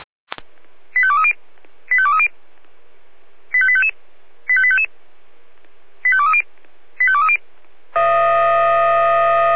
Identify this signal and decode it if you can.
Five-tone selective call (ZVEI), heard as quick runs of five tones, tones of about 70 ms. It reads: 85230 85230 85855 85855 85230 85230
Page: /496/SDR/ZVEI/85852/2021.07.09/